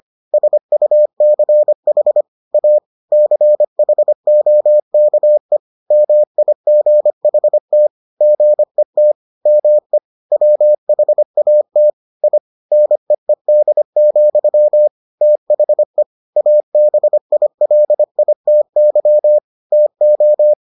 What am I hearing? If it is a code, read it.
Morse code, heard as a keyed tone, about 25 wpm, 600 Hz. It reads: SUCH A CHOKE MIGHT GET ME WHAT I NEED, THE ABILITY TO